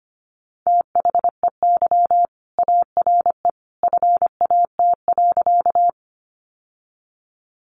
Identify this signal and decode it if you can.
Morse code, heard as a keyed tone, about 25 wpm, 700 Hz. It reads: THEY ARE FAT.
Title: Sentences from Top 600 Words 25wpm